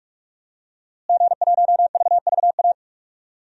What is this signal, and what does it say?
Morse code, heard as a keyed tone, about 46 wpm, 700 Hz. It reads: G1VVA
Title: DXCC-100 Call Signs - 5 Characters - Repeated 2x - 45wpm